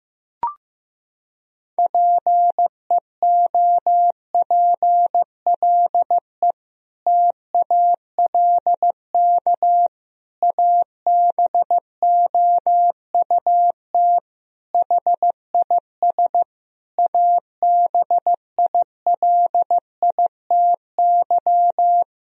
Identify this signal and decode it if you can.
Morse code, heard as a keyed tone, about 16 wpm, 700 Hz. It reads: PEOPLE TALK ABOUT HIS ABILITY
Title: Sentences from Top 1000 Words - Review Part 3 - 15wpm